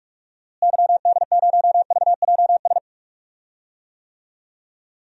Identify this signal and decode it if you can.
Morse code, heard as a keyed tone, about 42 wpm, 700 Hz. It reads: YD0VJS
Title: DXCC-100 Call Signs - 6 Characters - Repeated 2x - 45wpm